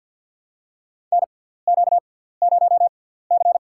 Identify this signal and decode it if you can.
Morse code, heard as a keyed tone, about 48 wpm, spaced as for 17 wpm, 700 Hz. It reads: NQ0C
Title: USA Call Signs - 4 Characters - 6x Character Spacing - 50wpm